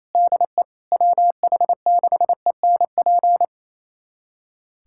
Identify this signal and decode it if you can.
Morse code, heard as a keyed tone, about 27 wpm, 700 Hz. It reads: DE WH6ENP
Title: USA Callsign DE Callsign - 6 Characters - 28wpm